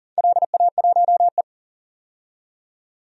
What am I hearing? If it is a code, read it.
Morse code, heard as a keyed tone, about 41 wpm, 700 Hz. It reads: LA1E
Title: DXCC-100 Call Signs - 4 Characters - 40wpm